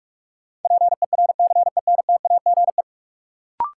Morse code, heard as a keyed tone, about 43 wpm, 700 Hz. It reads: PERCENTAGE
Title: Sentences from Top 1300 Words - New Word Drill - 45wpm